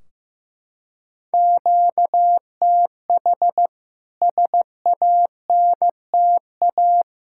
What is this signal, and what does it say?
Morse code, heard as a keyed tone, about 15 wpm, 700 Hz. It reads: QTH SANTA
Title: K1USN SST QSO Practice 15wpm